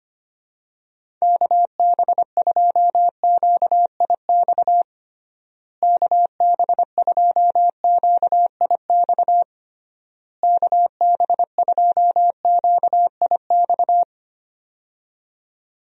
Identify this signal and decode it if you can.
Morse code, heard as a keyed tone, about 25 wpm, 700 Hz. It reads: KB2QIX KB2QIX KB2QIX
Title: USA Call Signs - 6 Characters - Repeated 3x - 25wpm